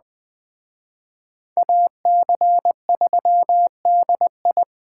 Morse code, heard as a keyed tone, about 22 wpm, 700 Hz. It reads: AC3DI